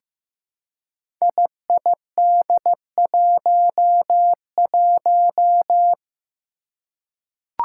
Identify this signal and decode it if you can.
Morse code, heard as a keyed tone, about 15 wpm, 700 Hz. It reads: IID11
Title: Five Challenging Letter-Number Combinations 15wpm